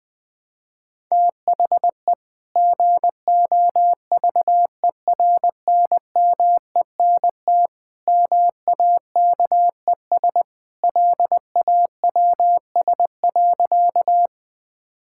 Morse code, heard as a keyed tone, about 19 wpm, 700 Hz. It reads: THE GOVERNMENT MAKES LAWS.
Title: Sentences with 2-Word Phrases - 20wpm